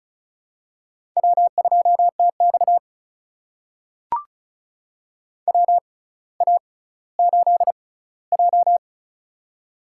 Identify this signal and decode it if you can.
Morse code, heard as a keyed tone, about 36 wpm, spaced as for 18 wpm, 700 Hz. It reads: W2TX WA8J
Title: USA Call Signs - 4 Characters - 6x Character Spacing - 35wpm